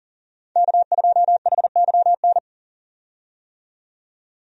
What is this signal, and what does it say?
Morse code, heard as a keyed tone, about 40 wpm, 700 Hz. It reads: K2HYN
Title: USA Call Signs - 5 Characters - 40wpm